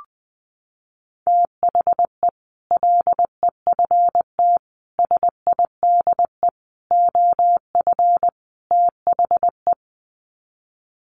Morse code, heard as a keyed tone, about 20 wpm, 700 Hz. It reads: THE LEFT SIDE OF THE
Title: Sets of 5 Words Encore 20wpm